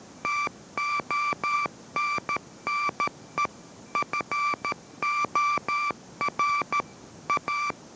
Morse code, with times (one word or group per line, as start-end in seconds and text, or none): 0.2-3.5 TONNE
4.0-6.8 FOR
7.3-7.7 A